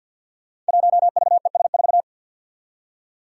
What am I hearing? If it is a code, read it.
Morse code, heard as a keyed tone, about 47 wpm, 700 Hz. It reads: 1VES4
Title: Five Challenging Letter-Number Combinations 50wpm